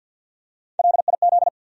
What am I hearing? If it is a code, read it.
Morse code, heard as a keyed tone, about 50 wpm, 700 Hz. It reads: LIZ
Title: USA Common Names - Max 3 Letters - 50wpm